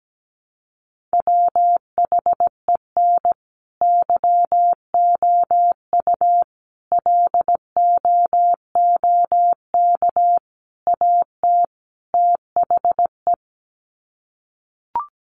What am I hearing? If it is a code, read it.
Morse code, heard as a keyed tone, about 17 wpm, 700 Hz. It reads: WHEN YOU LOOK AT THE
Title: Sets of 5 Words 17wpm